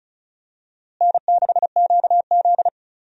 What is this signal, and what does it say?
Morse code, heard as a keyed tone, about 34 wpm, 700 Hz. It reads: N6QZ
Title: USA Call Signs - 4 Characters - Repeated 2x - 35wpm